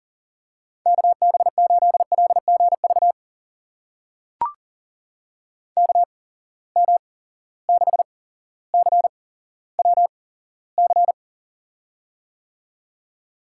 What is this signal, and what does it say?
Morse code, heard as a keyed tone, about 38 wpm, spaced as for 17 wpm, 700 Hz. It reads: KB8LGV KM6CWC
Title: USA Call Signs - 6 Characters - 8x Character Spacing - 40wpm